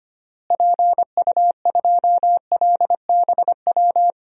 Morse code, heard as a keyed tone, about 25 wpm, 700 Hz. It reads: PU2LBW
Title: DXCC-100 Call Signs - 6 Characters - Repeated 3x - 25wpm